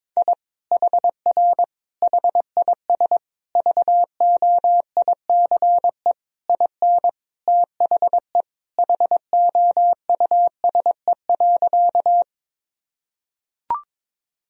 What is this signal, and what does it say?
Morse code, heard as a keyed tone, about 22 wpm, 700 Hz. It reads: I HR HIS VOICE IN THE HOUSE.
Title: Sentences from Top 500 Words and Abbreviations 22wpm